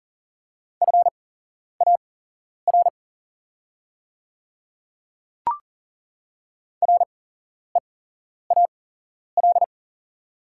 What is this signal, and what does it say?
Morse code, heard as a keyed tone, about 40 wpm, spaced as for 11 wpm, 690 Hz. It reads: FAR REAL
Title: 8x Character Spacing - Top 300 Words - 40wpm